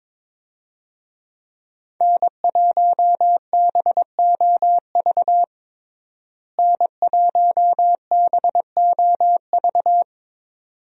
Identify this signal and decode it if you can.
Morse code, heard as a keyed tone, about 22 wpm, 700 Hz. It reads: N1BOV N1BOV
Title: USA Call Signs - 5 Characters - Repeated 2x - 22wpm